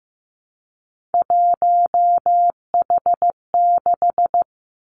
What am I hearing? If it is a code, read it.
Morse code, heard as a keyed tone, about 15 wpm, 700 Hz. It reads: 1H6
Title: Three Challenging Letter-Number Combinations 15wpm